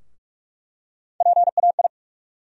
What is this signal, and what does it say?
Morse code, heard as a keyed tone, about 46 wpm, 700 Hz. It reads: PAI
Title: USA Call Sign Postfixes - 3 Letters 45wpm